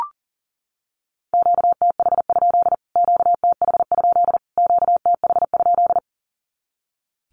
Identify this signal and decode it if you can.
Morse code, heard as a keyed tone, about 40 wpm, 700 Hz. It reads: QTH? QTH? QTH?
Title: Q-Codes Repeated 3x 40wpm